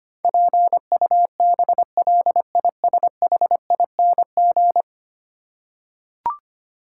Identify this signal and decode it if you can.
Morse code, heard as a keyed tone, about 25 wpm, 690 Hz. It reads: PUBLISHING
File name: 10 Letter Words - First 3 Letters Given - 25wpm